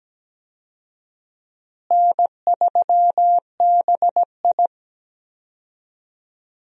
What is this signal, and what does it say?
Morse code, heard as a keyed tone, about 17 wpm, 700 Hz. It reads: N3BI
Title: USA Call Signs - 4 Characters - Repeated 2x - 17wpm